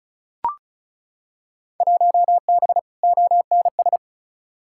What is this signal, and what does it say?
Morse code, heard as a keyed tone, about 36 wpm, 700 Hz. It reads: 1B ONS
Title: Field Day Exchanges Encore - 35wpm